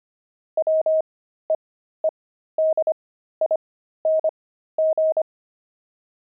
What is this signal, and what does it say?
Morse code, heard as a keyed tone, about 25 wpm, spaced as for 13 wpm, 630 Hz. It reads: WEEDING
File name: pangram-fs-13-cs-25-beehive